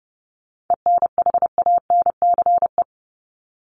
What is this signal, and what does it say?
Morse code, heard as a keyed tone, about 30 wpm, 700 Hz. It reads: ENHANCE
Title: Top 1251-1500 Words 30wpm